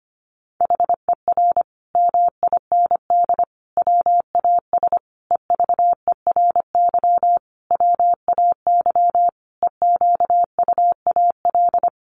Morse code, heard as a keyed tone, about 25 wpm, 700 Hz. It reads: HER MIND WAS EVERY WAY EQUAL
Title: Sentences from Top 1400 Words 25wpm